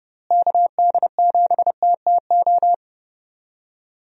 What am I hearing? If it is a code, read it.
Morse code, heard as a keyed tone, about 30 wpm, 700 Hz. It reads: KD7TTO
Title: USA Call Signs - 6 Characters - Repeated 3x - 30wpm